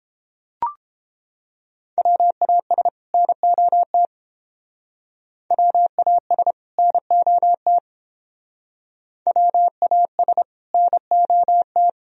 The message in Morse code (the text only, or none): WAS NOT WAS NOT WAS NOT